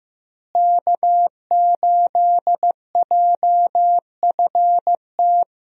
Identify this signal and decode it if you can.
Morse code, heard as a keyed tone, about 15 wpm, 700 Hz. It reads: K8JFT
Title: USA Call Signs - 5 Characters - Repeated 2x - 15wpm